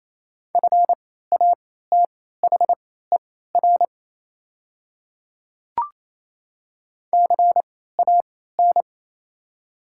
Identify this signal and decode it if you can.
Morse code, heard as a keyed tone, about 28 wpm, spaced as for 16 wpm, 700 Hz. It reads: FATHER CAN
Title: 3x Character Spacing - Top 300 Words - 28wpm